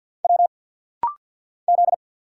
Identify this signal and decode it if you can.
Morse code, heard as a keyed tone, about 49 wpm, 700 Hz. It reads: W Z
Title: Single Letter-Number Alt - Rapid-Fire 50wpm